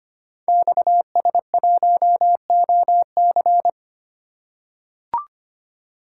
Morse code, heard as a keyed tone, about 25 wpm, 700 Hz. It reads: XS1OC
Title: Code Groups - 5 Letter:Numbers - 25wpm